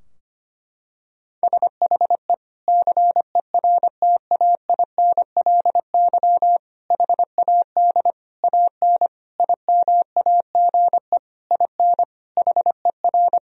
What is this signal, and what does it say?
Morse code, heard as a keyed tone, about 25 wpm, 700 Hz. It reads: SHE CERTAINLY HAD AN IMAGE IN HER
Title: Sentences from Top 1000 Words - Review Part 4 - 25wpm